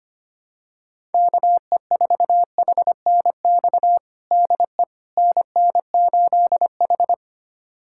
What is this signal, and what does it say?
Morse code, heard as a keyed tone, about 25 wpm, 700 Hz. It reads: KE4HNX DE NN8H
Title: USA Callsign DE Callsign 25wpm